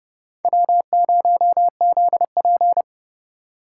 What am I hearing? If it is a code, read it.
Morse code, heard as a keyed tone, about 30 wpm, 700 Hz. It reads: W0ZP
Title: K1USN SST QSO Practice - 30wpm